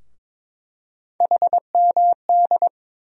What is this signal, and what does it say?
Morse code, heard as a keyed tone, about 22 wpm, 700 Hz. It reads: HMD
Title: USA Call Sign Postfixes - 3 Letters 22wpm